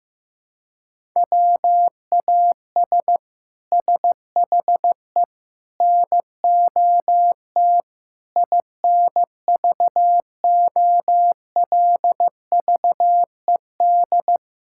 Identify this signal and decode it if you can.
Morse code, heard as a keyed tone, about 15 wpm, 700 Hz. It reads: WAS SHE NOT INVOLVED
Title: Sentences from Top 1000 Words - Review Part 5 - 15wpm